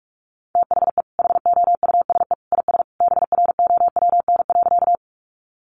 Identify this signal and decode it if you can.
Morse code, heard as a keyed tone, about 44 wpm, 700 Hz. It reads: THE HOUSE IS BROWN.
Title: Sentences from Top 1200 Words 45wpm